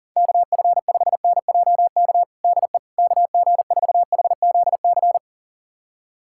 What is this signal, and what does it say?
Morse code, heard as a keyed tone, about 37 wpm, 700 Hz. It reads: KF5NJK DE KG4HZC
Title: USA Callsign DE Callsign - 6 Characters - 40wpm